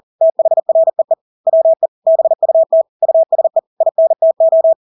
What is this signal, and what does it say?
Morse code, heard as a keyed tone, about 40 wpm, 650 Hz. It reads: THREE WE BUT USE INTO
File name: Engelske ord 40-40 wpm